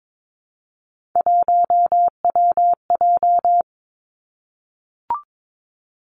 Morse code, heard as a keyed tone, about 22 wpm, 700 Hz. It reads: 1WJ
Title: Three Challenging Letter-Number Combinations 22wpm